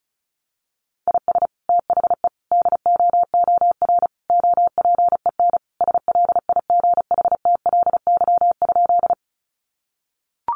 Morse code, heard as a keyed tone, about 34 wpm, 700 Hz. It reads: IS THE DOOR OPEN SLIGHTLY?
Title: Sentences from Top 1300 Words 35wpm